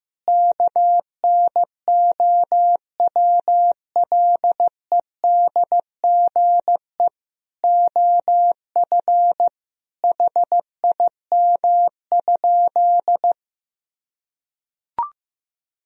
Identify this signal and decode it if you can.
Morse code, heard as a keyed tone, about 15 wpm, 700 Hz. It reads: KNOWLEDGE OF HIM?
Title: Sentences from Top 1000 Words - Review Part 3 - 15wpm